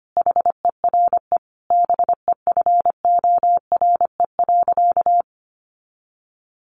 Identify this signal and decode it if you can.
Morse code, heard as a keyed tone, about 25 wpm, 700 Hz. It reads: HERE BEFORE.
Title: Additional Easy Phrases 25wpm